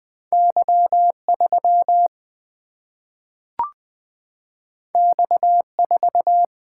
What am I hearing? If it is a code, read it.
Morse code, heard as a keyed tone, about 20 wpm, 700 Hz. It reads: Y3 X4
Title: Single-Letter Single-Number 20wpm